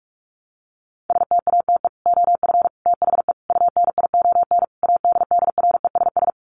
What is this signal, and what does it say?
Morse code, heard as a keyed tone, about 44 wpm, 700 Hz. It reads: STATE OF THE UNION ADDRESS
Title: Sets of 5 Words Encore 45wpm